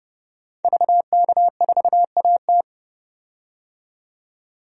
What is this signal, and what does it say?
Morse code, heard as a keyed tone, about 32 wpm, 700 Hz. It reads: VK4AT
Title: DXCC-100 Call Signs - 5 Characters - 30wpm